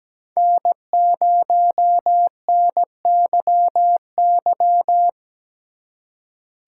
Morse code, heard as a keyed tone, about 17 wpm, 700 Hz. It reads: N0NYY
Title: USA Call Signs - 5 Characters - 3x Character Spacing - 17wpm